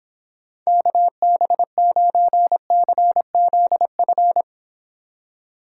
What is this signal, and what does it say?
Morse code, heard as a keyed tone, about 25 wpm, 700 Hz. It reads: KB9CZF